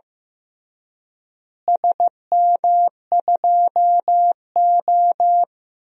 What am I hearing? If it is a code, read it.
Morse code, heard as a keyed tone, about 15 wpm, 700 Hz. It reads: SM2O
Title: DXCC-100 Call Signs - 4 Characters - Repeated 2x - 15wpm